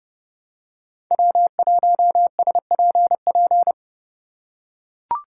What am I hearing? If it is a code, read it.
Morse code, heard as a keyed tone, about 30 wpm, 700 Hz. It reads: W1SPP
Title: USA Call Signs - 5 Characters - 2x Character Spacing - 30wpm